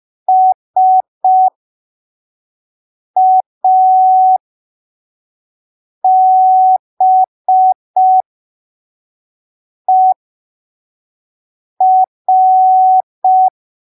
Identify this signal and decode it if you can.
Morse code, heard as a keyed tone, about 5 wpm, spaced as for 3 wpm, 750 Hz. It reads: SABER
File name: arquivos / texto / texto 15 05ppm